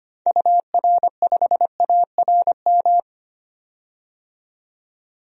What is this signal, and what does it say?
Morse code, heard as a keyed tone, about 25 wpm, 700 Hz. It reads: UR5ARM
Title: DXCC-100 Call Signs - 6 Characters - 25wpm